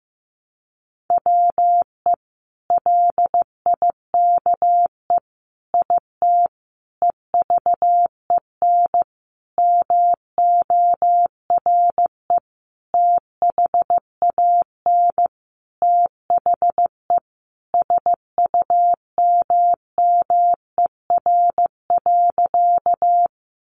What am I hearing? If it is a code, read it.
Morse code, heard as a keyed tone, about 15 wpm, 700 Hz. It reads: WE LIKE IT EVEN MORE THAN THE SUMMER.
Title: Sentences from Top 1100 Words 15wpm